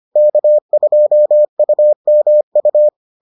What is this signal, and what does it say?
Morse code, heard as a keyed tone, about 25 wpm, 600 Hz. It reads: K2UMU